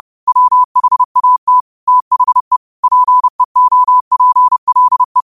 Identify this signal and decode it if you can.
Morse code, heard as a keyed tone, about 30 wpm, 1000 Hz. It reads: WHAT THE PEOPLE